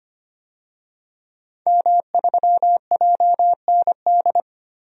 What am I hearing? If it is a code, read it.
Morse code, heard as a keyed tone, about 25 wpm, 700 Hz. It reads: M3JND
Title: DXCC-100 Call Signs - 5 Characters - Repeated 3x - 25wpm